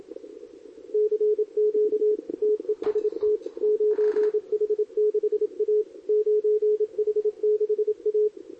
Morse code, heard as a keyed tone, about 29 wpm, 410 Hz, with radio noise. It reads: CQ TEST 9H6A 9H6A
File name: CQWWCW2023 / SP